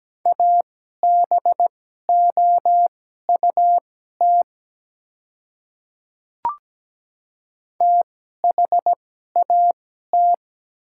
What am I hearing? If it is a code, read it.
Morse code, heard as a keyed tone, about 17 wpm, spaced as for 12 wpm, 700 Hz. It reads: ABOUT THAT
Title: 2x Character Spacing - Top 100 Words - 17wpm